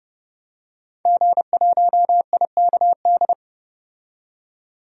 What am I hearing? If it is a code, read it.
Morse code, heard as a keyed tone, about 30 wpm, 690 Hz. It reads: G1IKD
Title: DXCC-100 Call Signs - 5 Characters - 30wpm